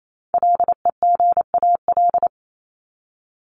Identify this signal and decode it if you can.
Morse code, heard as a keyed tone, about 28 wpm, 700 Hz. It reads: LEGAL
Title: Top 1001-1250 Words 28wpm